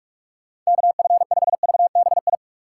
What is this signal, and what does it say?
Morse code, heard as a keyed tone, about 43 wpm, 700 Hz. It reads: KF5VBI